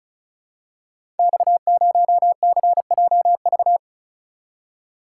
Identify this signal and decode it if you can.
Morse code, heard as a keyed tone, about 36 wpm, 700 Hz. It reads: X0CJV
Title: Code Groups - 5 Letter:Numbers - 35wpm